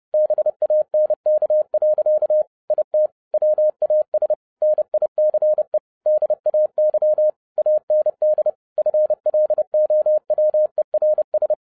Morse code, heard as a keyed tone, about 30 wpm, 610 Hz. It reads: BANK. IT WAS NICE DAY AND FLOWERS
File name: 2011-07-mp3 / n55 BACK FROM A BIG ACCIDENT 30wpm